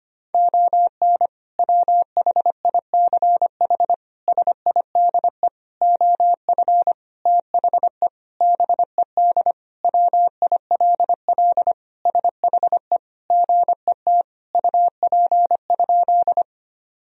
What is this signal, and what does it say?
Morse code, heard as a keyed tone, about 25 wpm, 700 Hz. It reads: ON WHICH SIDE OF THE BED WILL SHE GET UP?
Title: Sentences from Top 1000 Words - Review Part 1 - 25wpm